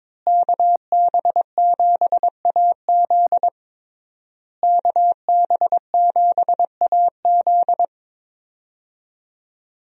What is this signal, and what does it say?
Morse code, heard as a keyed tone, about 21 wpm, 700 Hz. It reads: KB7AZ KB7AZ
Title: USA Call Signs - 5 Characters - Repeated 3x - 22wpm